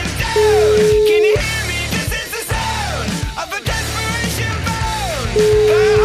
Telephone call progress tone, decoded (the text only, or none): ringback tone